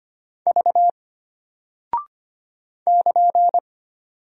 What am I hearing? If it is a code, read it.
Morse code, heard as a keyed tone, about 25 wpm, 700 Hz. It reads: V <KN>
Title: Single Character (Letter:Number:Punctuation:Prosign) Alt 25wpm